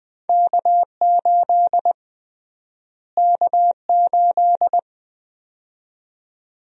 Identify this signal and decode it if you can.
Morse code, heard as a keyed tone, about 20 wpm, 700 Hz. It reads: K8 K8
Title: USA Call Sign Prefixes - Repeated 2x - 20wpm